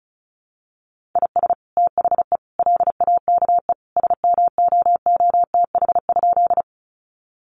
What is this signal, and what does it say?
Morse code, heard as a keyed tone, about 34 wpm, 700 Hz. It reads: IS THE LAKE SMOOTH?
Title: Sentences from Top 1400 Words 35wpm